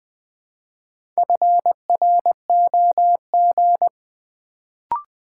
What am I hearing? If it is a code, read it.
Morse code, heard as a keyed tone, about 20 wpm, 700 Hz. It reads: FROG
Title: Build A Word - 2 Letters At A Time - 4-Letter Words - 20wpm